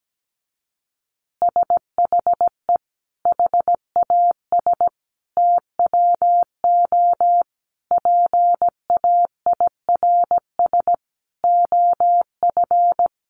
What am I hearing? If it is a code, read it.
Morse code, heard as a keyed tone, about 17 wpm, 700 Hz. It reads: SHE HAS TWO PAIRS OF
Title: Additional Easy Phrases 17wpm